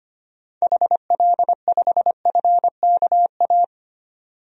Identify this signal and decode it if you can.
Morse code, heard as a keyed tone, about 25 wpm, 700 Hz. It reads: HL5FKA
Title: DXCC-100 Call Signs - 6 Characters - 25wpm